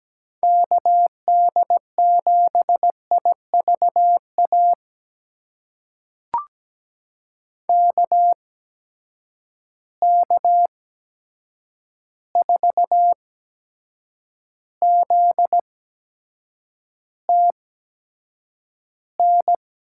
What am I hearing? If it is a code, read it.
Morse code, heard as a keyed tone, about 17 wpm, spaced as for 7 wpm, 700 Hz. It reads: KD7IVA KK4ZTN
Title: USA Call Signs - 6 Characters - 8x Character Spacing - 17wpm